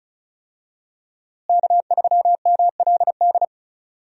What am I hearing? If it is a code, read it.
Morse code, heard as a keyed tone, about 36 wpm, 700 Hz. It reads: K3MLD
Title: USA Call Signs - 5 Characters - Repeated 2x - 35wpm